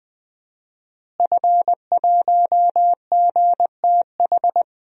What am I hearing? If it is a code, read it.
Morse code, heard as a keyed tone, about 20 wpm, 700 Hz. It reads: F1GTH
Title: DXCC-100 Call Signs - 5 Characters - Repeated 3x - 20wpm